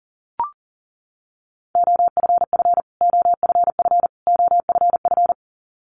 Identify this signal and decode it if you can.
Morse code, heard as a keyed tone, about 40 wpm, 700 Hz. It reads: OFF OFF OFF
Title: Top 300 Words Repeated 40wpm